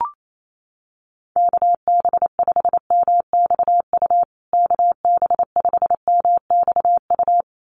Morse code, heard as a keyed tone, about 28 wpm, 700 Hz. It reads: KB5MXU KB5MXU
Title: USA Call Signs Repeated 2x 28wpm